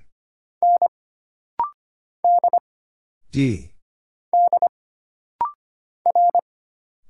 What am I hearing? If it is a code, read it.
Morse code, heard as a keyed tone, about 25 wpm, 700 Hz. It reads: N D D R
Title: Single Letters - Rapid-Fire 25wpm